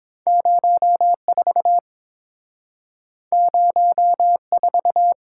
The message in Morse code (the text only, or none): 04 04